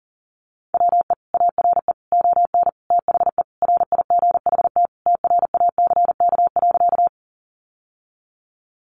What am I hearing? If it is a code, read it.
Morse code, heard as a keyed tone, about 40 wpm, 700 Hz. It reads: WE ARE ON THE RIGHT TRACK.
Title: Sentences from Top 1100 Words 40wpm